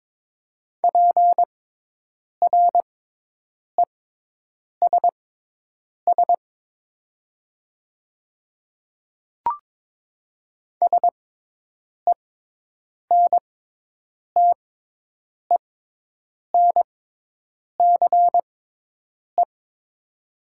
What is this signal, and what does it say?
Morse code, heard as a keyed tone, about 22 wpm, spaced as for 8 wpm, 700 Hz. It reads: PRESS SENTENCE
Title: 6x Character Spacing - Top 300 Words - 22wpm